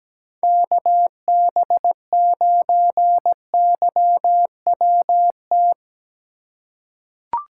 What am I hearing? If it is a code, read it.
Morse code, heard as a keyed tone, about 17 wpm, 700 Hz. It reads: KB9YWT
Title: USA Call Signs - 6 Characters - 3x Character Spacing - 17wpm